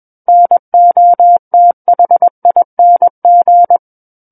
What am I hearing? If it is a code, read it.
Morse code, heard as a keyed tone, about 21 wpm, 700 Hz. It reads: NOTHING
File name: Book One / 12-Chapter 12 - 21 WPM